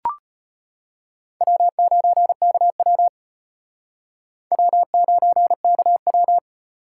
Morse code, heard as a keyed tone, about 32 wpm, 700 Hz. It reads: W9KW W9KW